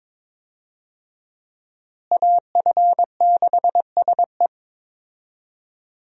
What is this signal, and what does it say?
Morse code, heard as a keyed tone, about 22 wpm, 700 Hz. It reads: AF6SE
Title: USA Call Signs - 5 Characters - Repeated 3x - 22wpm